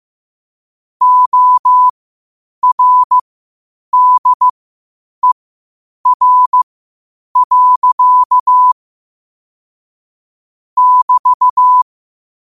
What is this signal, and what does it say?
Morse code, heard as a keyed tone, about 15 wpm, spaced as for 8 wpm, 1000 Hz. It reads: ORDER. <BT>
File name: 8wpm